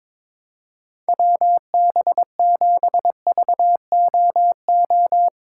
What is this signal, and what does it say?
Morse code, heard as a keyed tone, about 22 wpm, 700 Hz. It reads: WB7VOO
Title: USA Call Signs - 4x Character Spacing - 22wpm